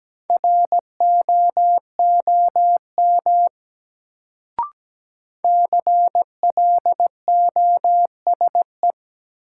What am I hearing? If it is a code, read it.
Morse code, heard as a keyed tone, about 17 wpm, 700 Hz. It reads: ROOM CLOSE
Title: Sentences from Top 300 Words - New Word Drill - 17wpm